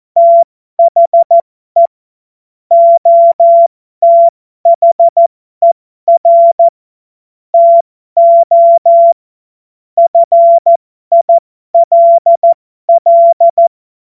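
Morse code, heard as a keyed tone, about 14 wpm, spaced as for 12 wpm, 680 Hz. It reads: THE OTHER TO FILL
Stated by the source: Morse code joke practice.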